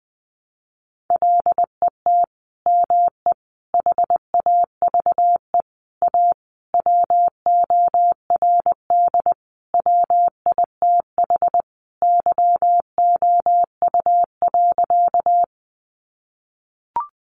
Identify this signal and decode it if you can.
Morse code, heard as a keyed tone, about 20 wpm, 700 Hz. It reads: LET ME HAVE A WORD WITH YOU.
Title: Sentences from Top 300 Words 20wpm